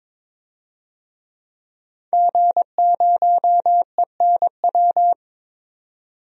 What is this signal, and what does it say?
Morse code, heard as a keyed tone, about 22 wpm, 700 Hz. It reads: G0ENW
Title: DXCC-100 Call Signs - 5 Characters - 22wpm